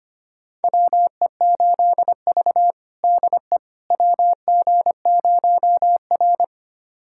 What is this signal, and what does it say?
Morse code, heard as a keyed tone, about 25 wpm, 700 Hz. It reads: WE8V DE WG0R
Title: USA Callsign DE Callsign - 4 Characters - 25wpm